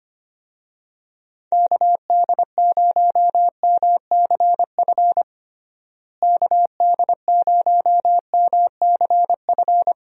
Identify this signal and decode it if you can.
Morse code, heard as a keyed tone, about 25 wpm, 700 Hz. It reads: KD0MCF KD0MCF
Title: USA Call Signs - 6 Characters - Repeated 2x - 25wpm